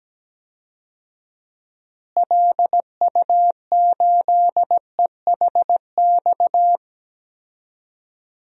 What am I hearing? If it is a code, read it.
Morse code, heard as a keyed tone, about 17 wpm, 700 Hz. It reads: LU8EHX
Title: DXCC-100 Call Signs - 6 Characters - Repeated 3x - 17wpm